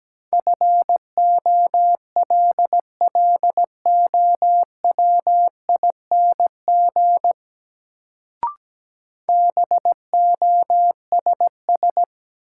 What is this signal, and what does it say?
Morse code, heard as a keyed tone, about 17 wpm, 700 Hz. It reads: FOLLOWING BOSS
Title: Sentences from Top 700 Words - New Word Drill - 17wpm